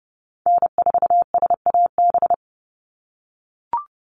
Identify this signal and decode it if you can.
Morse code, heard as a keyed tone, about 30 wpm, 700 Hz. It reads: N4SAB
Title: USA Call Signs 30wpm